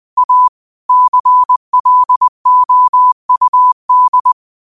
Morse code, heard as a keyed tone, about 20 wpm, 1000 Hz. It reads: A CLOUD
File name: qotdfiles / 20wpm